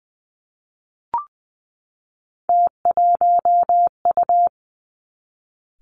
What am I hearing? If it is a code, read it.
Morse code, heard as a keyed tone, about 20 wpm, 700 Hz. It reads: T1U
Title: Three Challenging Letter-Number Combinations 20wpm